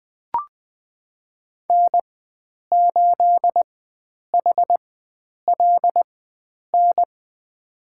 Morse code, heard as a keyed tone, about 20 wpm, spaced as for 9 wpm, 700 Hz. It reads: N8HLN
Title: USA Call Signs - 5 Characters - 4x Character Spacing - 20wpm